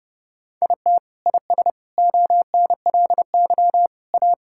Morse code, heard as a keyed tone, about 30 wpm, 700 Hz. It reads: IT IS ONLY A
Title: Sentences from Top 1600 Words 30wpm